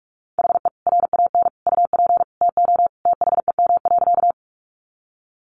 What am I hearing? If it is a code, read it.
Morse code, heard as a keyed tone, about 44 wpm, 700 Hz. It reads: HE RAN UP TO THEM.
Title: Sentences from Top 500 Words 45wpm